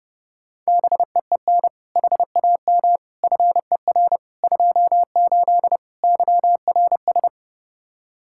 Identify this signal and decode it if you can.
Morse code, heard as a keyed tone, about 30 wpm, 700 Hz. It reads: BEEN HAM FER 28 YRS
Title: Practice QSO - Exchange 3 - More Info - 30wpm